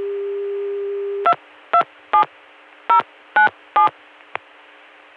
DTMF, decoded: 227097